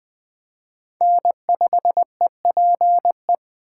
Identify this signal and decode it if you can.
Morse code, heard as a keyed tone, about 19 wpm, 700 Hz. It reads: N5EPE